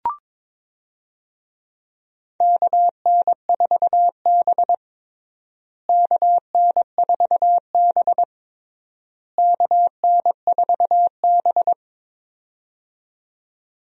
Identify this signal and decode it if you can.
Morse code, heard as a keyed tone, about 21 wpm, 700 Hz. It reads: KN4B KN4B KN4B
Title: USA Call Signs - 4 Characters - Repeated 3x - 22wpm